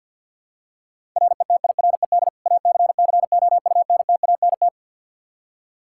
Morse code, heard as a keyed tone, about 52 wpm, 700 Hz. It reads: RETIRED ACCOUNTANT
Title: Practice QSO - Exchange 3 - More Info - 50wpm